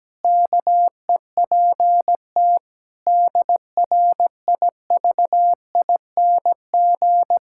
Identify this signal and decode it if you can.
Morse code, heard as a keyed tone, about 17 wpm, 700 Hz. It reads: KEPT DRIVING
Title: Pangram Challenge v2 - 17wpm